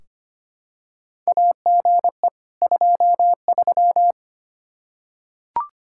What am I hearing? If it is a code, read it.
Morse code, heard as a keyed tone, about 25 wpm, 700 Hz. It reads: AGE 23
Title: K1USN SST QSO Practice 25wpm